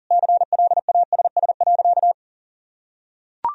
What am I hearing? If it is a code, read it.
Morse code, heard as a keyed tone, about 40 wpm, 700 Hz. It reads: CLASS.
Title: Sentences from Top 1000 Words - Review Part 4 - 40wpm